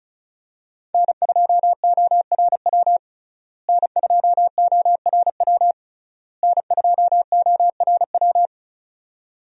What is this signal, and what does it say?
Morse code, heard as a keyed tone, about 36 wpm, 700 Hz. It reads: N2ORW N2ORW N2ORW
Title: USA Call Signs - 5 Characters - Repeated 3x - 35wpm